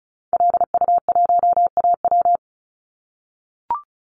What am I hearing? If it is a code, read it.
Morse code, heard as a keyed tone, about 36 wpm, 700 Hz. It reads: LU1AW
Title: Call Signs - 35wpm